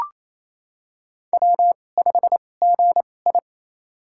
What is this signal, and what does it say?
Morse code, heard as a keyed tone, about 28 wpm, spaced as for 20 wpm, 700 Hz. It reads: W5GI
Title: USA Call Signs - 4 Characters - 2x Character Spacing - 28wpm